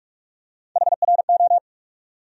Morse code, heard as a keyed tone, about 42 wpm, 700 Hz. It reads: HRO
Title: USA Call Sign Postfixes - 3 Letters 45wpm